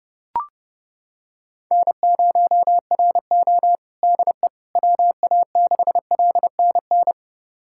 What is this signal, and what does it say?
Morse code, heard as a keyed tone, about 30 wpm, 700 Hz. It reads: N0RO DE WA6LNN
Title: USA Callsign DE Callsign 30wpm